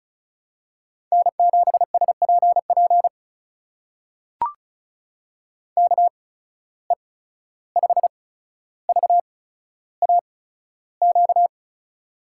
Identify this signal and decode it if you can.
Morse code, heard as a keyed tone, about 36 wpm, spaced as for 14 wpm, 700 Hz. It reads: N7SPP KE5VAQ
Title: USA Call Signs - 8x Character Spacing - 35wpm